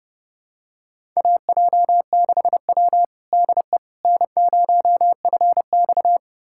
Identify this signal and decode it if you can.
Morse code, heard as a keyed tone, about 32 wpm, 700 Hz. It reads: AJ6W DE N0FX
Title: USA Callsign DE Callsign - 4 Characters - 30wpm